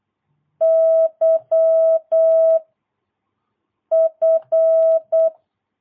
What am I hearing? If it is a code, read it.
Morse code, heard as a keyed tone, about 8 wpm, 660 Hz. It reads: Y F
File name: jh8cbh / 2022cw